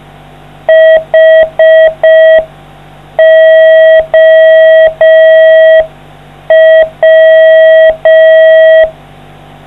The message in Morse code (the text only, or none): HOW